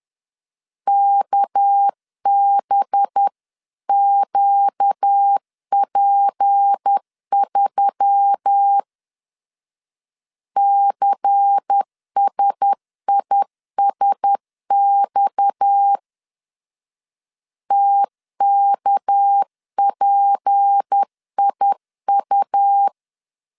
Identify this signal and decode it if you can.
Morse code, heard as a keyed tone, about 10 wpm, 800 Hz. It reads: KBQP3 CSISX TKPIU
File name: CW / DET / 1 6 1500C S B 10wpm 4137